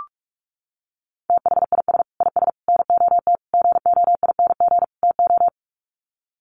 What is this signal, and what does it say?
Morse code, heard as a keyed tone, about 44 wpm, 700 Hz. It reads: THIS IS NOT GOING TO
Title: Sets of 5 Words Encore2 45wpm